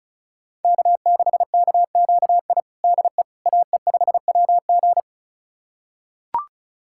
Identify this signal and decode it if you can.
Morse code, heard as a keyed tone, about 36 wpm, 690 Hz. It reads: K6KQI DE AE5WG
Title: USA Callsign DE Callsign - 5 Characters - 35wpm